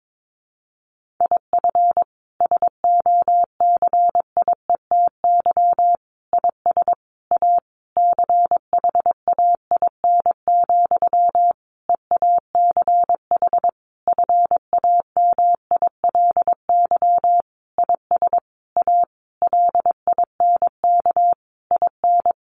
Morse code, heard as a keyed tone, about 22 wpm, 700 Hz. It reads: IF SOCIETY IS A CHAIN, EACH FAMILY IS A LINK IN
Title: Sentences from Top 1500 Words 22wpm